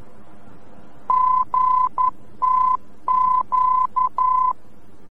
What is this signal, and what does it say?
Morse code, heard as a keyed tone, about 11 wpm, 1020 Hz. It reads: GTQ
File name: france / mx / GTQ-VOR-F